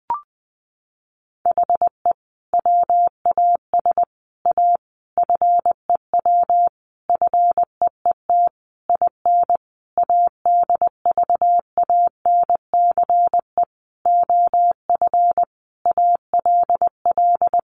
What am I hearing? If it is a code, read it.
Morse code, heard as a keyed tone, about 20 wpm, 700 Hz. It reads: HE WAS A FEW FEET IN ADVANCE OF ALL
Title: Sentences from Top 1400 Words 20wpm